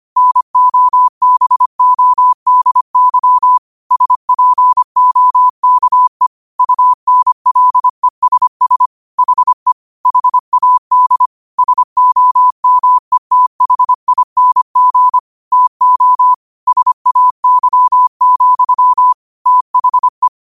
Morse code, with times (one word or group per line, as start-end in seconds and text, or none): 0.2-3.6 NOBODY
3.9-6.3 SPOKE
6.6-8.9 UNLESS
9.2-9.7 HE
10.1-11.3 HAD
11.6-15.2 SOMETHING
15.5-16.3 TO
16.7-19.1 SAY,
19.5-20.3 THE